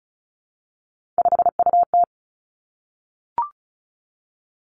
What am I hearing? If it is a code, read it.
Morse code, heard as a keyed tone, about 36 wpm, 700 Hz. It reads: 5UT
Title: Three Challenging Letter-Number Combinations 35wpm